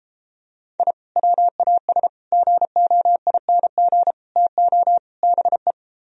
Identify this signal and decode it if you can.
Morse code, heard as a keyed tone, about 33 wpm, 700 Hz. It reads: I WAS GOING TO BE